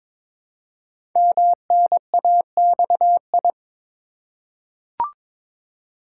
Morse code, heard as a keyed tone, about 22 wpm, 700 Hz. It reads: MNAXI
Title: Code Groups - 5 Letters - 22wpm